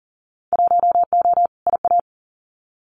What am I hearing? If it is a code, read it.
Morse code, heard as a keyed tone, about 39 wpm, 700 Hz. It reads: 1O IA
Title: Winter Field Day Exchanges 40wpm